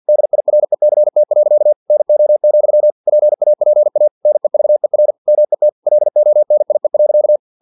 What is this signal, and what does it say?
Morse code, heard as a keyed tone, about 50 wpm, 600 Hz. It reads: DIRECT. NO, PAPA NEVER MET LONIE.
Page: Never 49wpm